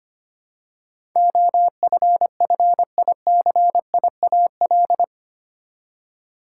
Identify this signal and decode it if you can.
Morse code, heard as a keyed tone, about 25 wpm, 700 Hz. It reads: OFFICIAL
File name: Eight Letter Words - 25wpm